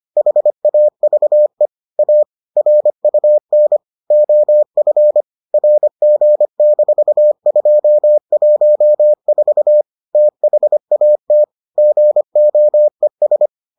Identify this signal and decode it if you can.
Morse code, heard as a keyed tone, about 25 wpm, 600 Hz. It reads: HAVE A RUN OF RG-214 THAT GOES